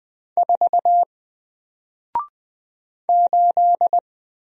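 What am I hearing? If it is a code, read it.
Morse code, heard as a keyed tone, about 20 wpm, 700 Hz. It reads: 4 8
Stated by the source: Single letter and number Morse drill at 20 wpm, from N4LCD.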